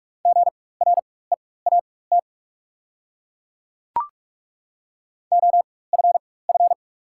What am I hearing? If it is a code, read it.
Morse code, heard as a keyed tone, about 43 wpm, spaced as for 21 wpm, 700 Hz. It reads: GREAT OFF
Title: 4x Character Spacing - Top 300 Words - 45wpm